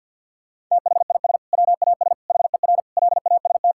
Morse code, heard as a keyed tone, about 50 wpm, 700 Hz. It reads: THIS WAS HER LAST
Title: Sentences from Top 1000 Words - Review Part 6 - 50wpm